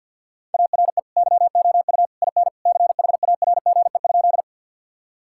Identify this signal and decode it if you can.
Morse code, heard as a keyed tone, about 50 wpm, 700 Hz. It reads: ARE YOU IN CHARGE?
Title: Sentences from Top 1000 Words - Review Part 6 - 50wpm